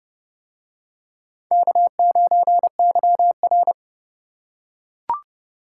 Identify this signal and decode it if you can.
Morse code, heard as a keyed tone, about 29 wpm, 700 Hz. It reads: K9YR